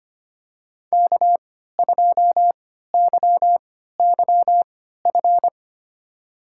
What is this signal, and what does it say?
Morse code, heard as a keyed tone, about 25 wpm, spaced as for 14 wpm, 700 Hz. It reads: K2YYF
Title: USA Call Signs - 5 Characters - 3x Character Spacing - 25wpm